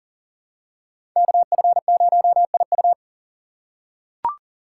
Morse code, heard as a keyed tone, about 40 wpm, 700 Hz. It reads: KF0IU
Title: USA Call Signs - 5 Characters - 6x Character Spacing - 40wpm